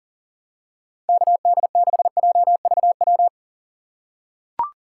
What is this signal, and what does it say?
Morse code, heard as a keyed tone, about 40 wpm, 700 Hz. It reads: KD6JVW
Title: USA Call Signs - 6x Character Spacing - 40wpm